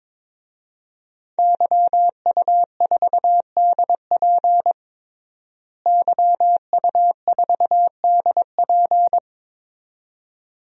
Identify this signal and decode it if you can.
Morse code, heard as a keyed tone, about 22 wpm, 700 Hz. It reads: YU4DP YU4DP
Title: DXCC-100 Call Signs - 5 Characters - Repeated 2x - 22wpm